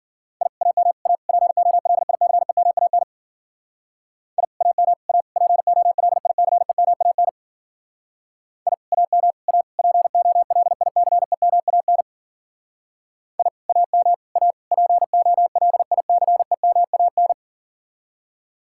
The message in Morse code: I AM A POLICEMAN I AM A POLICEMAN I AM A POLICEMAN I AM A POLICEMAN

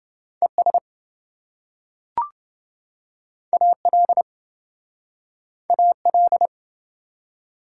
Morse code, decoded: ES AL AL

